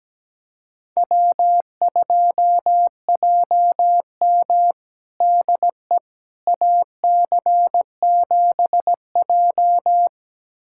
Morse code, heard as a keyed tone, about 17 wpm, 700 Hz. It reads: W2JM DE AC7J
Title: USA Callsign DE Callsign 17wpm